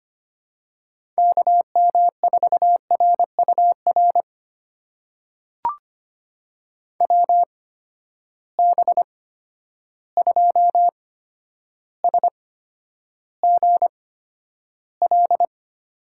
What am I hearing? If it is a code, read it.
Morse code, heard as a keyed tone, about 25 wpm, spaced as for 11 wpm, 700 Hz. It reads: KM4RUR WB2SGL
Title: USA Call Signs - 6 Characters - 8x Character Spacing - 25wpm